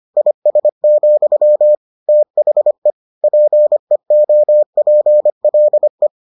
Morse code, heard as a keyed tone, about 25 wpm, 600 Hz. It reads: IS, THE PEOPLE